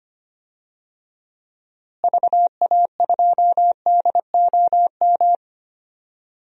Morse code, heard as a keyed tone, about 25 wpm, 700 Hz. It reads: VA2DOM
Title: DXCC-100 Call Signs - 6 Characters - Repeated 3x - 25wpm